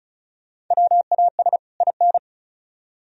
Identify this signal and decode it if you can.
Morse code, heard as a keyed tone, about 42 wpm, 700 Hz. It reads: WAS IN
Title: Speed-Racing - Sets of 2 Words 35wpm